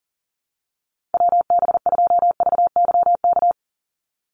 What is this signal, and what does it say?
Morse code, heard as a keyed tone, about 40 wpm, 700 Hz. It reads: WB2VYK
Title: Call Signs - 40wpm